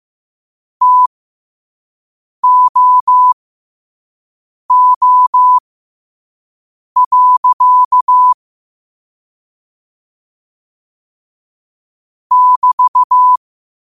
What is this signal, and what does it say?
Morse code, heard as a keyed tone, about 15 wpm, spaced as for 5 wpm, 1000 Hz. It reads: TOO. <BT>